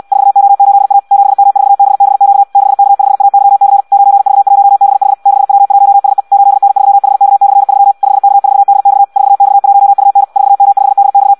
Morse code, heard as a keyed tone, about 74 wpm, 800 Hz. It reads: 4O8A 7M4BBQ 6K5AYC 9V1BH 6W1SE 8N3HK/3 5K4DX 4X1UN 4M5DX